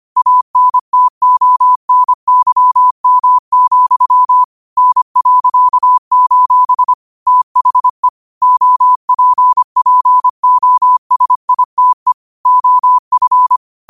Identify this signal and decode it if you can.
Morse code, heard as a keyed tone, about 25 wpm, 1000 Hz. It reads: ANTONYM, N.: THE OPPOSITE OF